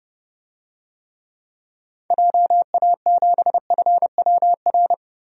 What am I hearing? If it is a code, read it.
Morse code, heard as a keyed tone, about 30 wpm, 700 Hz. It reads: JA7FWR
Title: DXCC-100 Call Signs - 6 Characters - 30wpm